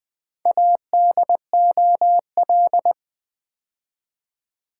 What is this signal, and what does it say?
Morse code, heard as a keyed tone, about 21 wpm, 700 Hz. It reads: ADOL
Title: Build A Word - 4 Letters At A Time - 10-Letter Words - 20wpm